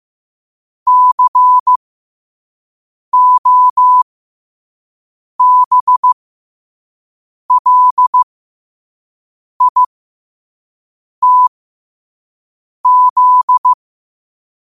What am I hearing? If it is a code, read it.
Morse code, heard as a keyed tone, about 15 wpm, spaced as for 5 wpm, 1000 Hz. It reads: COBLITZ
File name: qotdfiles / 5wpm